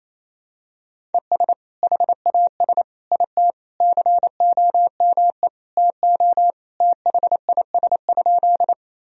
Morse code, heard as a keyed tone, about 28 wpm, 700 Hz. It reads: ES HAS IT COME TO THIS?
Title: Sentences from Top 100 Words and Abbreviations 28wpm